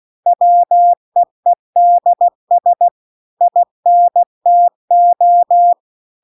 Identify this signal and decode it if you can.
Morse code, heard as a keyed tone, about 16 wpm, 700 Hz. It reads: WEEDS INTO
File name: CW-The-Big-Twoheaded-River-2-16WPM